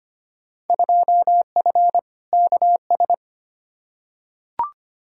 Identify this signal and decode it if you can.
Morse code, heard as a keyed tone, about 25 wpm, 700 Hz. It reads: 2F KS
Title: Field Day Exchanges Encore - 25wpm